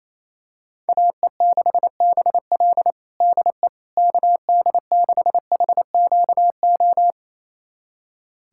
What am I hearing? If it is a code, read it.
Morse code, heard as a keyed tone, about 28 wpm, 700 Hz. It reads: AE6BL DE KD6HQO
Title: USA Callsign DE Callsign 28wpm